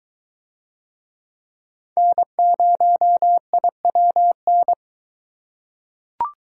Morse code, heard as a keyed tone, about 22 wpm, 700 Hz. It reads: N0IWN